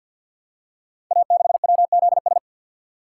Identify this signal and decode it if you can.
Morse code, heard as a keyed tone, about 48 wpm, 700 Hz. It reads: A6WZS
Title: Code Groups - 5 Letter:Numbers - 50wpm